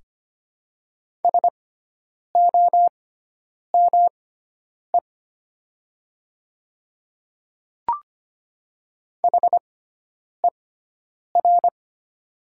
Morse code, heard as a keyed tone, about 25 wpm, spaced as for 9 wpm, 700 Hz. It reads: SOME HER
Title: 6x Character Spacing - Top 100 Words - 25wpm